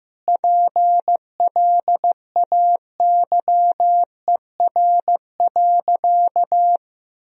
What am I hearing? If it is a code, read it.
Morse code, heard as a keyed tone, about 15 wpm, 700 Hz. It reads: PLAYER.
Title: Sentences from Top 1000 Words - Review Part 4 - 15wpm